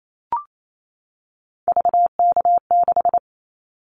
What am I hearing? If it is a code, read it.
Morse code, heard as a keyed tone, about 28 wpm, 700 Hz. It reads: VK6
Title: Callsign Prefixes for Top 50 DXCC - 28wpm